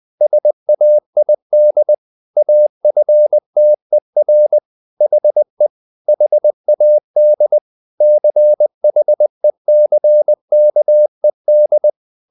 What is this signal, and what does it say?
Morse code, heard as a keyed tone, about 20 wpm, 600 Hz. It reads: SAID AFTER HE HAD CHECKED
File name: the old man and the sea part 2 20wpm